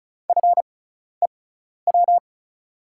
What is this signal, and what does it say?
Morse code, heard as a keyed tone, about 36 wpm, spaced as for 12 wpm, 700 Hz. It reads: FEW
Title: 6x Character Spacing - Top 300 Words - 35wpm